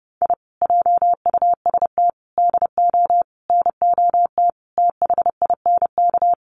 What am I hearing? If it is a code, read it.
Morse code, heard as a keyed tone, about 30 wpm, 700 Hz. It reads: I JUST DO NOT THINK
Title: Sets of 5 Words Encore 30wpm